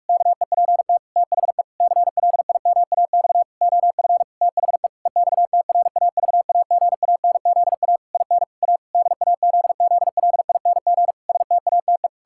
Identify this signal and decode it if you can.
Morse code, heard as a keyed tone, about 46 wpm, 700 Hz. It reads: KEPT THE CLIMAX OF THE EXTRAVAGANZA IN A DAZZLING STATE
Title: Pangram Challenge v2 - 45wpm